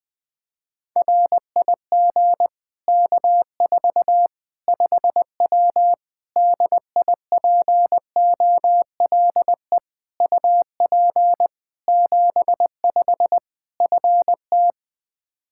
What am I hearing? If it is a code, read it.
Morse code, heard as a keyed tone, about 21 wpm, 700 Hz. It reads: RIG K4 5W DIPOLE UP 75 FT
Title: Practice QSO - Exchange 2 - Info - 20wpm